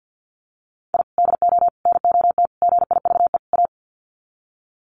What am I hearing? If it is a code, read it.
Morse code, heard as a keyed tone, about 50 wpm, 700 Hz. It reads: I DO NOT GIVE A
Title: Sets of 5 Words Encore2 50wpm